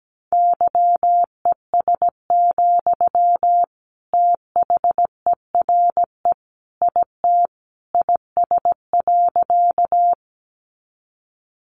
Morse code, heard as a keyed tone, about 17 wpm, 700 Hz. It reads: YES, THERE IT IS.
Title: Sentences from Top 500 Words 17wpm